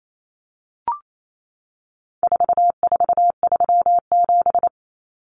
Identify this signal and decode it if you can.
Morse code, heard as a keyed tone, about 28 wpm, 700 Hz. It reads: 4437
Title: Numbers - 4-Digit 28wpm